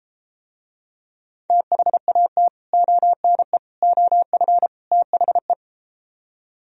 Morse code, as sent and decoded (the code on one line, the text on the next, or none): - .... .- - / --- -. / --- ..-. / - .... .
THAT ONE OF THE